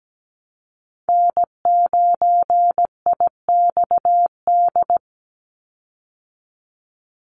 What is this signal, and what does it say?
Morse code, heard as a keyed tone, about 17 wpm, 700 Hz. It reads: N9IXD
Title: Call Signs Repeated 2x - 17wpm